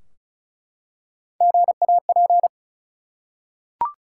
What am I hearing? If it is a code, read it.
Morse code, heard as a keyed tone, about 36 wpm, 700 Hz. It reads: GAP
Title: Three Letter Words - 35wpm